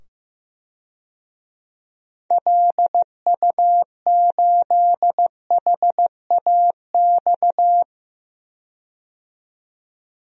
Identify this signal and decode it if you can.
Morse code, heard as a keyed tone, about 16 wpm, 700 Hz. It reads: LU8HAX
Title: DXCC-100 Call Signs - 6 Characters - 15wpm